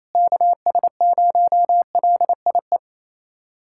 Morse code, heard as a keyed tone, about 28 wpm, 700 Hz. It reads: KS0LIE
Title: USA Callsign DE Callsign - 6 Characters - 28wpm